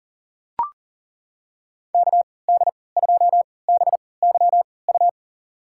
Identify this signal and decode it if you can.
Morse code, heard as a keyed tone, about 40 wpm, spaced as for 23 wpm, 700 Hz. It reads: KD2BYU
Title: USA Call Signs - 6 Characters - 3x Character Spacing - 40wpm